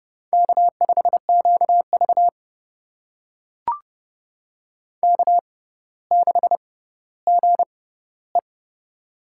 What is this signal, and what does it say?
Morse code, heard as a keyed tone, about 30 wpm, spaced as for 15 wpm, 700 Hz. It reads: K5QV K6GE
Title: USA Call Signs - 4 Characters - 6x Character Spacing - 30wpm